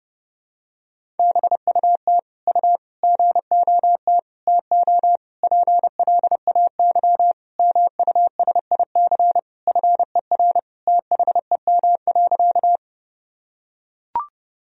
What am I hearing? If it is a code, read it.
Morse code, heard as a keyed tone, about 30 wpm, 700 Hz. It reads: BUT U GOT TO PLAY MUSIC FER THEM.
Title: Sentences from Top 300 Words and Abbreviations 30wpm